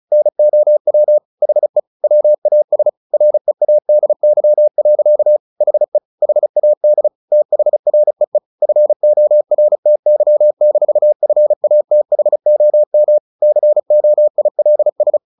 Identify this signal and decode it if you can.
Morse code, heard as a keyed tone, about 34 wpm, 600 Hz. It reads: NOW HE WAS READY. HE HAD THREE FORTY-FATHOM COILS
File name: the old man and the sea / the old man and the sea 1 35wpm